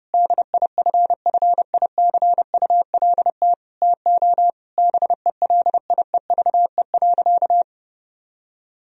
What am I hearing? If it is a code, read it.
Morse code, heard as a keyed tone, about 32 wpm, 700 Hz. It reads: DIFFICULT TO BELIEVE.
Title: Sentences from Top 1000 Words - Review Part 4 - 30wpm